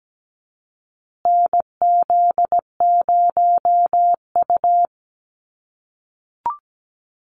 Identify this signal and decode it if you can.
Morse code, heard as a keyed tone, about 17 wpm, 700 Hz. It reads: NZ0U
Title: USA Call Signs Repeated 3x 17wpm